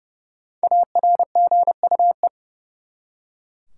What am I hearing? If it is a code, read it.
Morse code, heard as a keyed tone, about 30 wpm, 700 Hz. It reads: ARGUE
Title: Five Letter Words - 30wpm